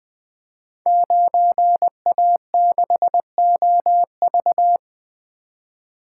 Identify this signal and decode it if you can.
Morse code, heard as a keyed tone, about 20 wpm, 700 Hz. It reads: 9A6OV
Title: Code Groups - 5 Letter:Numbers - 20wpm